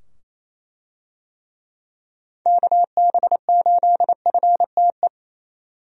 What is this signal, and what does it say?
Morse code, heard as a keyed tone, about 27 wpm, 700 Hz. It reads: KB8FTE
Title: USA Call Signs - 6 Characters - 28wpm